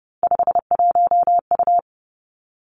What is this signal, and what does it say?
Morse code, heard as a keyed tone, about 30 wpm, 700 Hz. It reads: 51U
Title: Three Challenging Letter-Number Combinations 30wpm